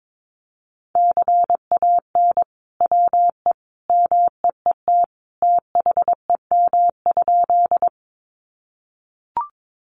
Morse code, heard as a keyed tone, about 22 wpm, 700 Hz. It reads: CAN WE MEET THEM?
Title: Sentences from Top 1100 Words 22wpm